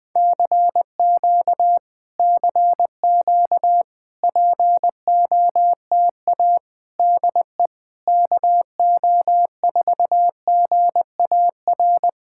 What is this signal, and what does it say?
Morse code, heard as a keyed tone, about 20 wpm, 700 Hz. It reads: CQ CQ POTA DE KO4GAR
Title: POTA QSO Practice - 20wpm